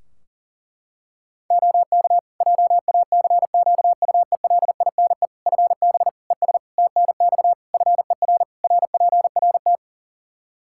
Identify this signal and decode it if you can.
Morse code, heard as a keyed tone, about 40 wpm, 700 Hz. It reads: OK JACQUELINE FB ES TNX FER RPRT
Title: K1USN SST QSO Practice 40wpm